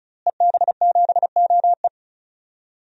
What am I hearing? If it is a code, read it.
Morse code, heard as a keyed tone, about 36 wpm, 700 Hz. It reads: EB7OE
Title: DXCC-100 Call Signs - 5 Characters - Repeated 2x - 35wpm